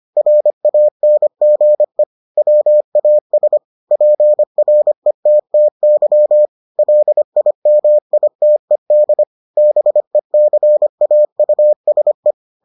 Morse code, heard as a keyed tone, about 25 wpm, 600 Hz. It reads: RANGE WAS PRETTY LIMITED BECAUSE